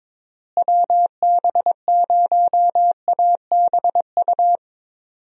Morse code, heard as a keyed tone, about 22 wpm, 700 Hz. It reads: WB0ABU
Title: USA Call Signs - 6 Characters - Repeated 3x - 22wpm